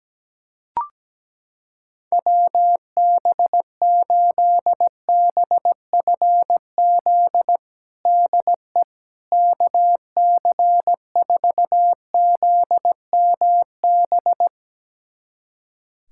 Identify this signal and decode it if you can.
Morse code, heard as a keyed tone, about 17 wpm, 700 Hz. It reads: WB8BFZ DE KC4ZMB
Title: USA Callsign DE Callsign - 6 Characters - 17wpm